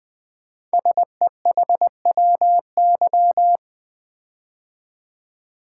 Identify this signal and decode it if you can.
Morse code, heard as a keyed tone, about 20 wpm, 700 Hz. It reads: SEHWY
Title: Code Groups - 5 Letters - 20wpm